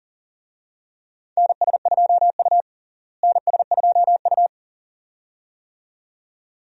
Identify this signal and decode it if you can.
Morse code, heard as a keyed tone, about 37 wpm, 700 Hz. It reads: NS2U NS2U
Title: USA Call Signs - 4 Characters - Repeated 2x - 40wpm